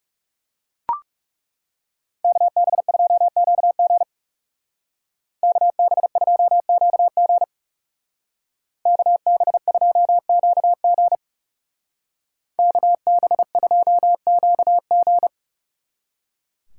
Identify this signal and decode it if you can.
Morse code, heard as a keyed tone, about 39 wpm, 700 Hz. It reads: KB2QG KB2QG KB2QG KB2QG